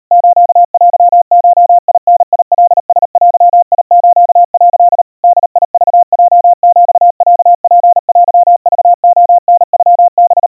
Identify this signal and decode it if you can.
Morse code (Japanese Wabun code), heard as a keyed tone, about 37 wpm, 700 Hz. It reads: ステゴダカラデス」 ボクヲヒロツテクレタノハ
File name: kawa / Morse sample / kamyu 140 cpm 700 Hz Crispy